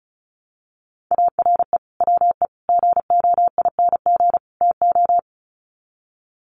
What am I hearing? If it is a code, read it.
Morse code, heard as a keyed tone, about 34 wpm, 700 Hz. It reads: ARE WE GOING TO
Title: Sets of 4 Words 35wpm